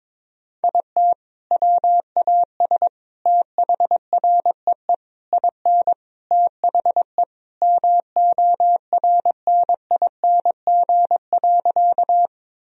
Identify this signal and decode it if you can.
Morse code, heard as a keyed tone, about 22 wpm, 700 Hz. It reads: IT WAS THREE IN THE MORNING.
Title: Sentences from Top 1000 Words - Review Part 1 - 22wpm